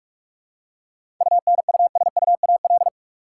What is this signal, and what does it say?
Morse code, heard as a keyed tone, about 47 wpm, 700 Hz. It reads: UNUSUAL
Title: Seven Letter Words - 45wpm